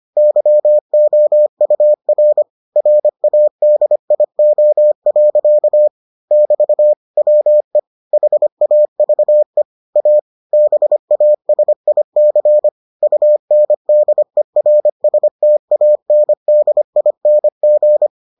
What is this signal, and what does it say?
Morse code, heard as a keyed tone, about 25 wpm, 600 Hz. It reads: YOUR RADIO. <BT> WE HAVE A BASIC UNDERSTANDING